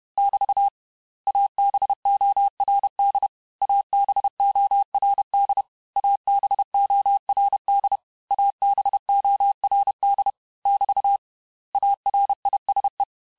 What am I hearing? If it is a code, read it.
Morse code, heard as a keyed tone, about 31 wpm, 800 Hz. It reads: <BT> ABORD ABORD ABORD ABORD <BT> ARISE
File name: cours-du-mercredi / audios / cours-861-texte-les-mots-vitesse-29MPM